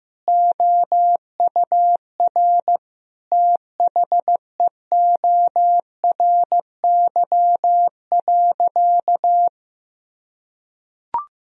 Morse code, heard as a keyed tone, about 15 wpm, 700 Hz. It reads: OUR THEORY.
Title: Sentences from Top 1000 Words - Review Part 3 - 15wpm